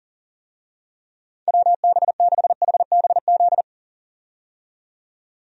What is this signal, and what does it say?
Morse code, heard as a keyed tone, about 40 wpm, 700 Hz. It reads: WB6HBZ
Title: USA Call Signs - 6 Characters - 40wpm